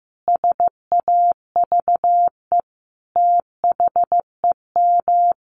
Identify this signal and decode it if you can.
Morse code, heard as a keyed tone, about 15 wpm, 700 Hz. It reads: SAVE THEM
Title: Sentences from Top 1100 Words 15wpm